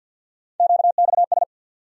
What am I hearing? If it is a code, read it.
Morse code, heard as a keyed tone, about 48 wpm, 700 Hz. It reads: QXS
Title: USA Call Sign Postfixes - 3 Letters 50wpm